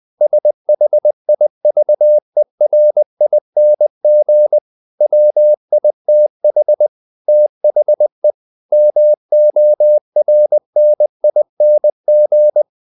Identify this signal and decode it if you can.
Morse code, heard as a keyed tone, about 20 wpm, 600 Hz. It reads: SHIVERING WITH THE MORNING
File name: the old man and the sea / the old man and the sea part 1 20wpm